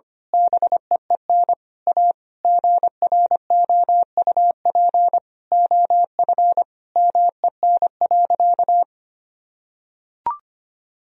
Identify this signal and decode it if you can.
Morse code, heard as a keyed tone, about 25 wpm, 700 Hz. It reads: BEEN A GROUP OF MEN.
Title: Sentences from Top 1000 Words - Review Part 1 - 25wpm